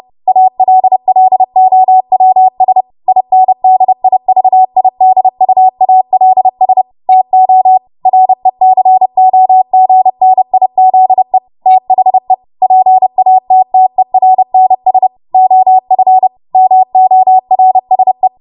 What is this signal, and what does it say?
Morse code, heard as a keyed tone, about 30 wpm, 750 Hz. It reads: ALLOWS INDIVIDUALS TO RECOGNIZE THE PATTERNS OF MORSE
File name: Morse / 30WPM